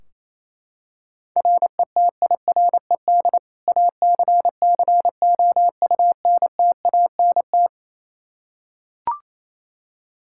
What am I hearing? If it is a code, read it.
Morse code, heard as a keyed tone, about 28 wpm, 700 Hz. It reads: RETIRED ACCOUNTANT
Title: Practice QSO - Exchange 3 - More Info - 28wpm